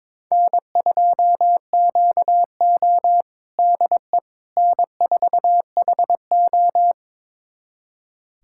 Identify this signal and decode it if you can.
Morse code, heard as a keyed tone, about 22 wpm, 700 Hz. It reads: N2QO DE N4HO
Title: USA Callsign DE Callsign - 4 Characters - 22wpm